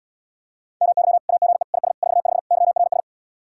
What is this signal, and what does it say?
Morse code, heard as a keyed tone, about 80 wpm, 700 Hz. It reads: MY AGE IS 45 YRS